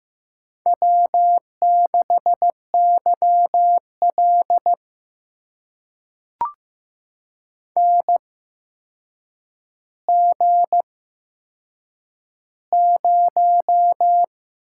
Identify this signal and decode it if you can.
Morse code, heard as a keyed tone, about 15 wpm, spaced as for 7 wpm, 700 Hz. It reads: W6YL NG0
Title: USA Call Signs - 4 Characters - 8x Character Spacing - 15wpm